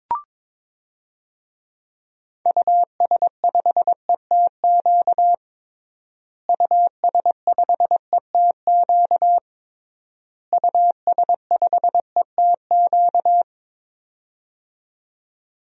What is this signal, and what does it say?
Morse code, heard as a keyed tone, about 22 wpm, 700 Hz. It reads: US5ETQ US5ETQ US5ETQ
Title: DXCC-100 Call Signs - 6 Characters - Repeated 3x - 22wpm